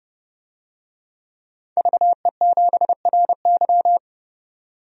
Morse code, heard as a keyed tone, about 30 wpm, 700 Hz. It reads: VE7RY
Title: DXCC-100 Call Signs - 5 Characters - 30wpm